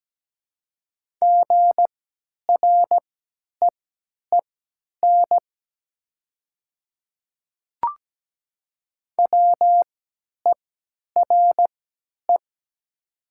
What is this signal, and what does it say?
Morse code, heard as a keyed tone, about 17 wpm, spaced as for 10 wpm, 700 Hz. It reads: GREEN WERE
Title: 3x Character Spacing - Top 500 Words - 17wpm